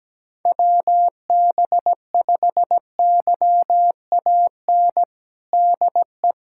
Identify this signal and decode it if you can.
Morse code, heard as a keyed tone, about 17 wpm, 700 Hz. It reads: WB5YAN DE
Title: USA Callsign DE Callsign 17wpm